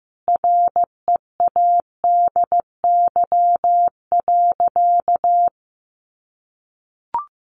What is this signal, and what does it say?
Morse code, heard as a keyed tone, about 15 wpm, 700 Hz. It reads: READY.
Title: Sentences from Top 1500 Words 15wpm